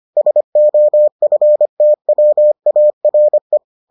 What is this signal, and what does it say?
Morse code, heard as a keyed tone, about 25 wpm, 600 Hz. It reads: SOFTWARE